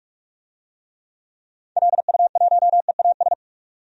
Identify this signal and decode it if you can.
Morse code, heard as a keyed tone, about 46 wpm, 690 Hz. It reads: LU1EAS
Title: DXCC-100 Call Signs - 6 Characters - 45wpm